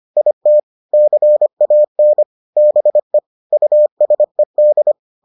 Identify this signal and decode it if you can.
Morse code, heard as a keyed tone, about 25 wpm, 600 Hz. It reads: IT CAN BE USED